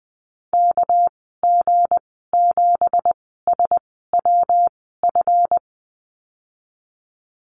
Morse code, heard as a keyed tone, about 20 wpm, spaced as for 15 wpm, 700 Hz. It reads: KG7SWF
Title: 213 Lesson 34g - Number 8 - Call Signs 20wpm